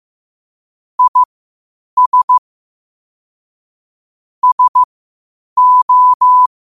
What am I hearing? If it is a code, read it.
Morse code, heard as a keyed tone, about 15 wpm, spaced as for 8 wpm, 1000 Hz. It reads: IS SO
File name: qotdfiles / 8wpm